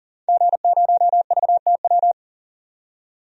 Morse code, heard as a keyed tone, about 40 wpm, 700 Hz. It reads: G0VTW
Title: DXCC-100 Call Signs - 5 Characters - 40wpm